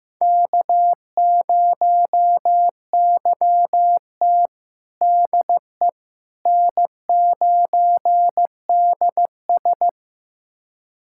Morse code, heard as a keyed tone, about 15 wpm, 700 Hz. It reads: K0YT DE N9DS
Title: USA Callsign DE Callsign - 4 Characters - 15wpm